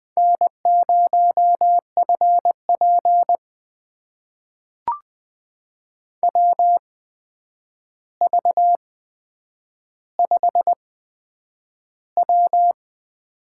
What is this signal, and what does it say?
Morse code, heard as a keyed tone, about 20 wpm, spaced as for 9 wpm, 700 Hz. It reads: N0FP WV5W
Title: USA Call Signs - 4 Characters - 8x Character Spacing - 20wpm